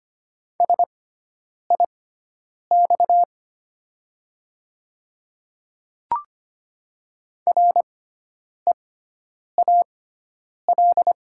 Morse code, heard as a keyed tone, about 25 wpm, spaced as for 9 wpm, 700 Hz. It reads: SIX REAL
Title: 6x Character Spacing - Top 500 Words - 25wpm